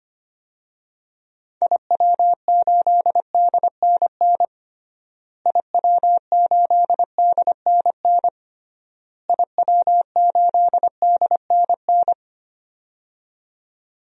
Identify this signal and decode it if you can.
Morse code, heard as a keyed tone, about 25 wpm, 700 Hz. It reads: IW8DNN IW8DNN IW8DNN
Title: DXCC-100 Call Signs - 6 Characters - Repeated 3x - 25wpm